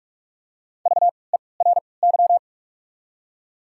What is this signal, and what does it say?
Morse code, heard as a keyed tone, about 44 wpm, spaced as for 25 wpm, 700 Hz. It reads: VERY
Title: 3x Character Spacing - Top 500 Words - 45wpm